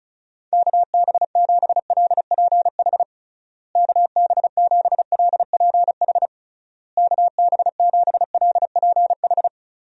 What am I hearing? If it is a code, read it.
Morse code, heard as a keyed tone, about 34 wpm, 700 Hz. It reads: KB7LPH KB7LPH KB7LPH
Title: USA Call Signs - 6 Characters - Repeated 3x - 35wpm